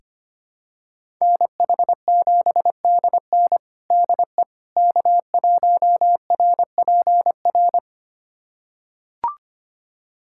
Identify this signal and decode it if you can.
Morse code, heard as a keyed tone, about 25 wpm, 700 Hz. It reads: NH7DN DE K1RPR
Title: USA Callsign DE Callsign 25wpm